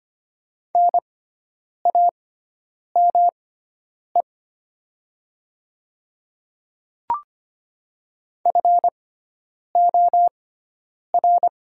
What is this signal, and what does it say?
Morse code, heard as a keyed tone, about 25 wpm, spaced as for 9 wpm, 700 Hz. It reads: NAME FOR
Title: 6x Character Spacing - Top 300 Words - 25wpm